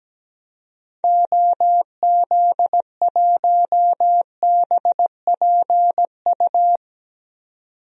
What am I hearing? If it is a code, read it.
Morse code, heard as a keyed tone, about 17 wpm, 700 Hz. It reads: OZ1BPU
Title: DXCC-100 Call Signs - 6 Characters - 17wpm